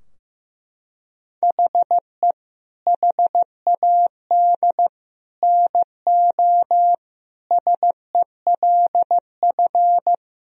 Morse code, heard as a keyed tone, about 16 wpm, 700 Hz. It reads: HE HAD NO SELF
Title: Sentences from Top 1000 Words - Review Part 3 - 15wpm